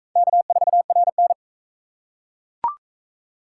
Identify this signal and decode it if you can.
Morse code, heard as a keyed tone, about 39 wpm, 700 Hz. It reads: K4RN